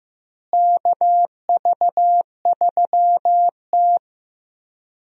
Morse code, heard as a keyed tone, about 15 wpm, 700 Hz. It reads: KV3T
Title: USA Call Signs - 4 Characters - Repeated 2x - 15wpm